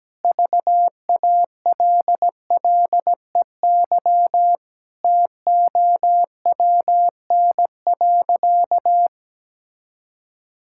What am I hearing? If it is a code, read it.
Morse code, heard as a keyed tone, about 17 wpm, 700 Hz. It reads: VALLEY TOWN.
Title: Pangram Challenge v2 - 17wpm